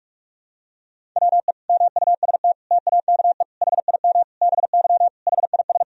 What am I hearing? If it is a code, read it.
Morse code, heard as a keyed tone, about 41 wpm, 700 Hz. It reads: WE MUST TAKE HIM BY HIS
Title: Sentences from Top 1000 Words - Review Part 1 - 45wpm